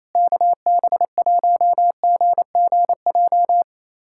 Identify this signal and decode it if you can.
Morse code, heard as a keyed tone, about 28 wpm, 700 Hz. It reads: KB1GGJ
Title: USA Callsign DE Callsign - 6 Characters - 28wpm